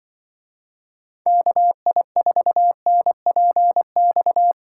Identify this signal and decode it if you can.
Morse code, heard as a keyed tone, about 23 wpm, 700 Hz. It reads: KI4NPX
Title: Speed Racing - USA 2x3 Call Signs 20wpm